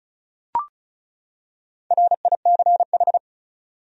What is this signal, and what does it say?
Morse code, heard as a keyed tone, about 35 wpm, 700 Hz. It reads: RICH
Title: USA Common Names - Max 5 Letters - 35wpm